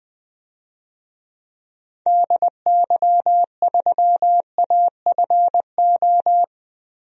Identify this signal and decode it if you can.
Morse code, heard as a keyed tone, about 20 wpm, 700 Hz. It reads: DY3AFO
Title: DXCC-100 Call Signs - 6 Characters - Repeated 2x - 20wpm